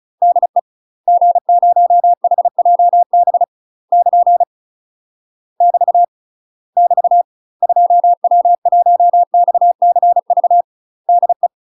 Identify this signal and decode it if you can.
Morse code, heard as a keyed tone, about 35 wpm, 700 Hz. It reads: DE G0HJB <KN> <BT> <BT> 2W1XCV DE
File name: UK MASTER QSOs TRAINING / 35WPM / UK MASTER QSOs 0034